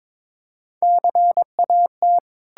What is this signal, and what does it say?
Morse code, heard as a keyed tone, about 22 wpm, 700 Hz. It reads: CAT
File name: Difficult Words with Hints - 22wpm